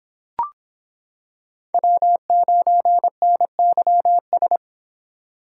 Morse code, heard as a keyed tone, about 26 wpm, 700 Hz. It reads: W9NYS